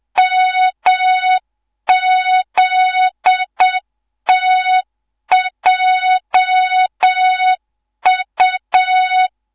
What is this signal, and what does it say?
Morse code, heard as a keyed tone, about 7 wpm, 3700 Hz, with noise. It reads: MZTJU